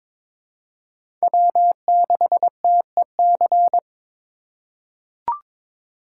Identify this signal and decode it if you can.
Morse code, heard as a keyed tone, about 22 wpm, 700 Hz. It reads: W6TEC
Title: USA Call Signs - 5 Characters - 3x Character Spacing - 22wpm